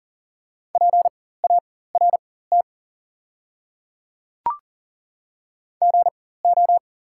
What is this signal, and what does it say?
Morse code, heard as a keyed tone, about 40 wpm, spaced as for 19 wpm, 690 Hz. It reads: PART GO
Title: 4x Character Spacing - Top 500 Words - 40wpm